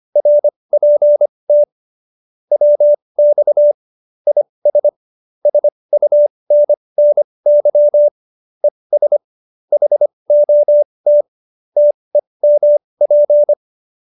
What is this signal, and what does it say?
Morse code, heard as a keyed tone, about 25 wpm, spaced as for 20 wpm, 590 Hz. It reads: RPT WX IS SUNNY ES HOT TEMP
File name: QSO316 20